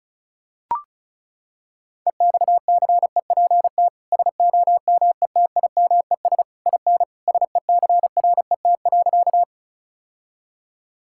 Morse code, heard as a keyed tone, about 34 wpm, 700 Hz. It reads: EXCEPT SOMETIMES IN SECRET.
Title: Sentences from Top 1600 Words 35wpm